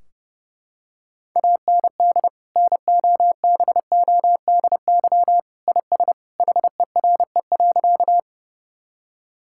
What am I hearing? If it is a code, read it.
Morse code, heard as a keyed tone, about 31 wpm, 700 Hz. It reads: AND NOBODY IS HERE.
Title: Sentences from Top 1600 Words 30wpm